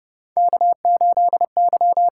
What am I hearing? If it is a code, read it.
Morse code, heard as a keyed tone, about 30 wpm, 700 Hz. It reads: K8Y
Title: USA Call Signs - 6x Character Spacing - 30wpm